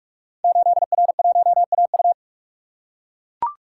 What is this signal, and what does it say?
Morse code, heard as a keyed tone, about 44 wpm, 700 Hz. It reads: 8R1AU
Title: Code Groups - 5 Letter:Numbers - 45wpm